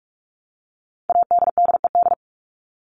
Morse code, heard as a keyed tone, about 44 wpm, 700 Hz. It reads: ADDED
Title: ED Suffix - Top 100 Words with Base Words 45wpm